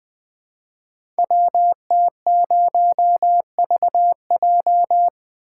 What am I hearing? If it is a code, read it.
Morse code, heard as a keyed tone, about 20 wpm, 700 Hz. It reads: WT0VJ
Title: Code Groups - 5 Letter:Numbers - 20wpm